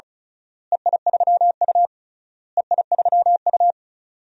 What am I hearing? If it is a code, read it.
Morse code, heard as a keyed tone, about 34 wpm, 700 Hz. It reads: EI3U EI3U
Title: DXCC-100 Call Signs - 4 Characters - Repeated 3x - 35wpm